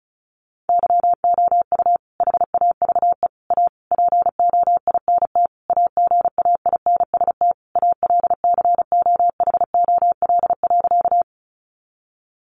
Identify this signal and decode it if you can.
Morse code, heard as a keyed tone, about 34 wpm, 700 Hz. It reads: YOU HAVE A POINT AGAINST ALCOHOL.
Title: Sentences from Top 1400 Words 35wpm